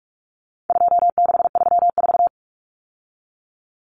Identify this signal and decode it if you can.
Morse code, heard as a keyed tone, about 44 wpm, 700 Hz. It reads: 2634
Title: Numbers - 4-Digit 45wpm